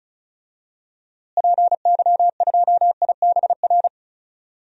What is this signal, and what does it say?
Morse code, heard as a keyed tone, about 34 wpm, 700 Hz. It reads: PY2IBR
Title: DXCC-100 Call Signs - 6 Characters - Repeated 2x - 35wpm